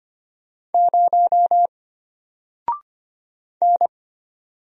Morse code, heard as a keyed tone, about 25 wpm, 700 Hz. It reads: 0 N
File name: Single-Letter-Number-25wpm